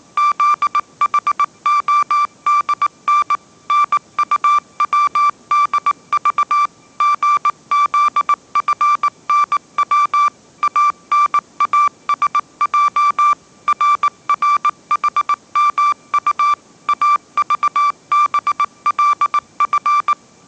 Morse code, heard as a keyed tone, about 19 wpm, 1170 Hz. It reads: ZHODN NUWDV GZFNW ANASJ RRHMU AVBLF
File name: mrx083 20wpm